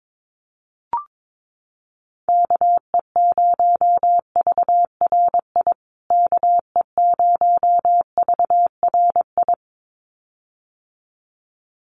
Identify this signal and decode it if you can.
Morse code, heard as a keyed tone, about 22 wpm, 700 Hz. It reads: KE0VRI KE0VRI
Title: USA Call Signs Repeated 2x 22wpm